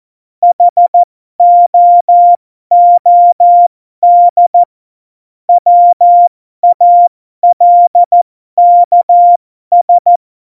Morse code, heard as a keyed tone, about 14 wpm, spaced as for 12 wpm, 700 Hz. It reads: HOOD WALKS
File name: Morsekurs / Witze